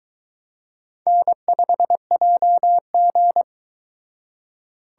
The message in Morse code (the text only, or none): N5JG